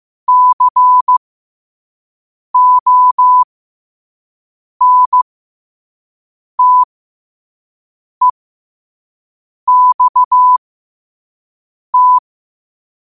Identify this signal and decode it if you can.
Morse code, heard as a keyed tone, about 15 wpm, spaced as for 5 wpm, 1000 Hz. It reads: CONTEXT
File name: qotdfiles / 5wpm